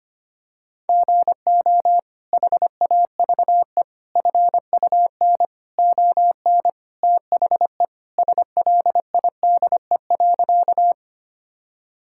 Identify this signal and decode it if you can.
Morse code, heard as a keyed tone, about 25 wpm, 700 Hz. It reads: GO HAVE FUN ON THE SLIDE.
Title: Sentences from Top 1600 Words 25wpm